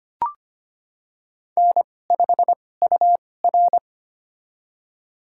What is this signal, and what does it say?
Morse code, heard as a keyed tone, about 25 wpm, spaced as for 18 wpm, 700 Hz. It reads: N5UR
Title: USA Call Signs - 4 Characters - 2x Character Spacing - 25wpm